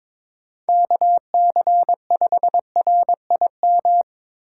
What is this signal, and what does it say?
Morse code, heard as a keyed tone, about 23 wpm, 700 Hz. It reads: KC5RIM
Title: USA Call Signs - 6 Characters - 3x Character Spacing - 22wpm